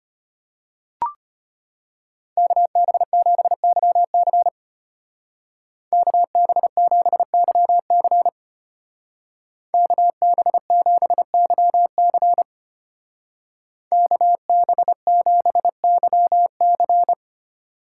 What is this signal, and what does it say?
Morse code, heard as a keyed tone, about 32 wpm, 700 Hz. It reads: KB7YC KB7YC KB7YC KB7YC